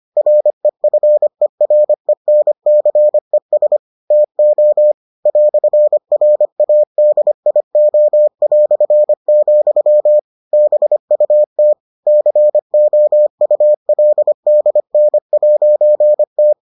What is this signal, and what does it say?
Morse code, heard as a keyed tone, about 25 wpm, 600 Hz. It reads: REFERENCES TO "RADIO", BUT COULDN'T